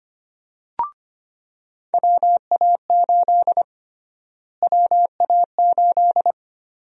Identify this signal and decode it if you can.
Morse code, heard as a keyed tone, about 25 wpm, 700 Hz. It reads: WA8 WA8
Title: USA Call Sign Prefixes - Repeated 2x - 25wpm